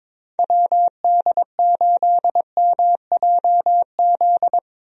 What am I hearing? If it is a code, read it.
Morse code, heard as a keyed tone, about 22 wpm, 700 Hz. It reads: WD8MJZ
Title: USA Call Signs - 6 Characters - Repeated 3x - 22wpm